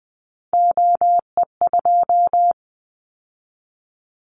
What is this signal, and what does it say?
Morse code, heard as a keyed tone, about 20 wpm, 700 Hz. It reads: OE2
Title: Callsign Prefixes for Top 100 DXCC - 20wpm